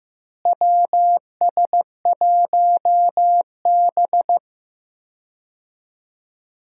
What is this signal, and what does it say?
Morse code, heard as a keyed tone, about 15 wpm, 700 Hz. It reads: WS1B
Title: USA Call Signs - 4 Characters - Repeated 2x - 15wpm